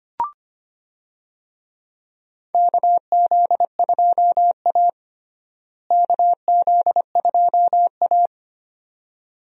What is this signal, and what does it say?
Morse code, heard as a keyed tone, about 25 wpm, 700 Hz. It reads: KZ2A KZ2A
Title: USA Call Signs - 4 Characters - Repeated 2x - 25wpm